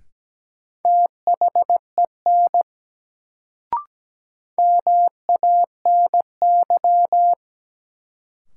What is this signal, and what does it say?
Morse code, heard as a keyed tone, about 17 wpm, 700 Hz. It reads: THEN MANY
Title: Sentences from Top 100 Words - New Word Drill - 17wpm